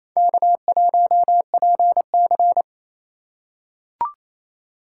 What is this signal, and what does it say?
Morse code, heard as a keyed tone, about 28 wpm, 700 Hz. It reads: K1PC
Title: USA Call Signs - 4 Characters - 3x Character Spacing - 28wpm